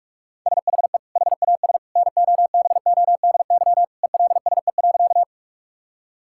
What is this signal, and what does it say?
Morse code, heard as a keyed tone, about 46 wpm, 700 Hz. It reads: SHE HAS NOBODY ELSE.
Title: Sentences from Top 1600 Words 45wpm